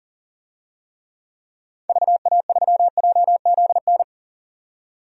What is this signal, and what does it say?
Morse code, heard as a keyed tone, about 37 wpm, 700 Hz. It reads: VA3JZN
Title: DXCC-100 Call Signs - 6 Characters - 40wpm